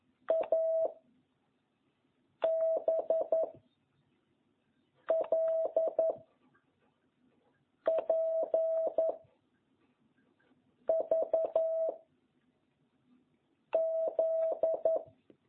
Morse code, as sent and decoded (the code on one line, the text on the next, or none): .- / -... / .-.. / .--. / ...- / --..
A B L P V Z